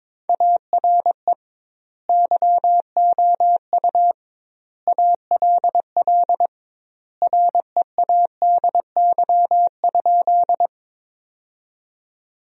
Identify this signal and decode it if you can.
Morse code, heard as a keyed tone, about 22 wpm, 700 Hz. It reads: ARE YOU ALL READY?
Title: 2x Word-Spacing - Sentences from Top 400 Words - 22wpm